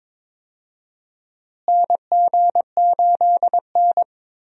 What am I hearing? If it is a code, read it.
Morse code, heard as a keyed tone, about 22 wpm, 700 Hz. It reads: NG8N
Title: USA Call Signs - 4 Characters - Repeated 2x - 22wpm